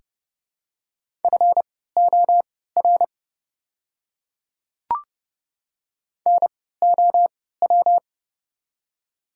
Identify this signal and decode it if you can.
Morse code, heard as a keyed tone, about 30 wpm, spaced as for 17 wpm, 700 Hz. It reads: FOR NOW
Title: 3x Character Spacing - Top 100 Words - 30wpm